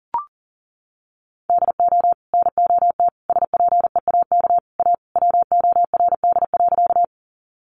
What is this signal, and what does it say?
Morse code, heard as a keyed tone, about 40 wpm, 700 Hz. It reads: DO NOT SPEAK A WORD.
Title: Sentences from Top 1100 Words 40wpm